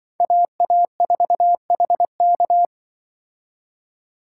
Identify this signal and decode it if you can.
Morse code, heard as a keyed tone, about 25 wpm, 700 Hz. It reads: AA4HK
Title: Speed Racing - USA Call Signs 20wpm